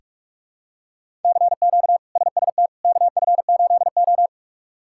Morse code, heard as a keyed tone, about 46 wpm, 700 Hz. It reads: CQ SST KF8O
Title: K1USN SST QSO Practice - 45wpm